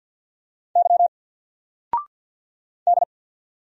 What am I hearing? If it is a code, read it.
Morse code, heard as a keyed tone, about 50 wpm, 700 Hz. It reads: Y D
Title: Single Letter-Number 50wpm